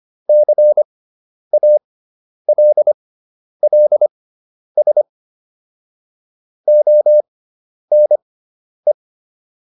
Morse code, heard as a keyed tone, about 25 wpm, spaced as for 10 wpm, 600 Hz. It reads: CALLS ONE